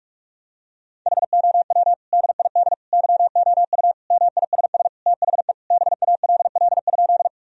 Morse code, heard as a keyed tone, about 41 wpm, 700 Hz. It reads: HOW DID YOU MISS THE BALL?
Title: Sentences from Top 1000 Words - Review Part 3 - 45wpm